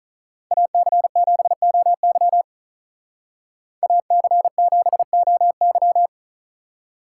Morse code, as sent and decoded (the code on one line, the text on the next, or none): .- -.-. --... --- -.-- / .- -.-. --... --- -.--
AC7OY AC7OY